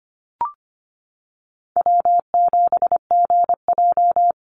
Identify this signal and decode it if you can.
Morse code, heard as a keyed tone, about 25 wpm, 700 Hz. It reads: W7GJ
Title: NA SOTA Chaser Callsigns 25wpm